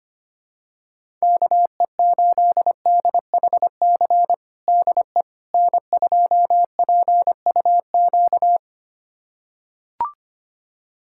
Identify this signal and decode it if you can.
Morse code, heard as a keyed tone, about 25 wpm, 700 Hz. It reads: KE8DHC DE N2PUQ
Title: USA Callsign DE Callsign 25wpm